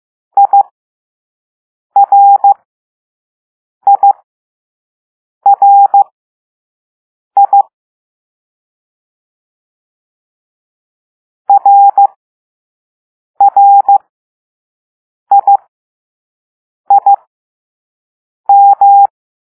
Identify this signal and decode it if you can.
Morse code, heard as a keyed tone, about 15 wpm, spaced as for 5 wpm, 800 Hz. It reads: IRIRI RRIIM
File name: Morse / Level2B